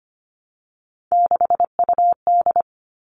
Morse code, heard as a keyed tone, about 25 wpm, 700 Hz. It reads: 6UD
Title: All Cut Numbers - 3-Digit 25wpm